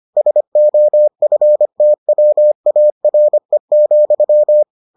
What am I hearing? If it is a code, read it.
Morse code, heard as a keyed tone, about 25 wpm, 600 Hz. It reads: SOFTWARE,